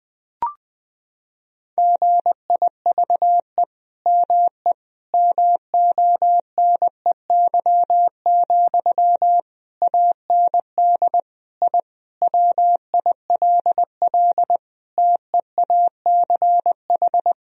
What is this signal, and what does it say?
Morse code, heard as a keyed tone, about 20 wpm, 700 Hz. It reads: GIVE ME MONEY, AND I WILL TEACH
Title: Sentences from Top 1000 Words - Review Part 2 - 20wpm